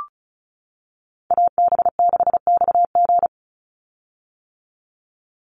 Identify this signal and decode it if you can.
Morse code, heard as a keyed tone, about 34 wpm, 700 Hz. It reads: AB6XG
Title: Call Signs - 35wpm